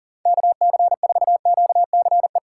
Morse code, heard as a keyed tone, about 40 wpm, 700 Hz. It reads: KC4QCE
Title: USA Call Signs - 6 Characters - 6x Character Spacing - 40wpm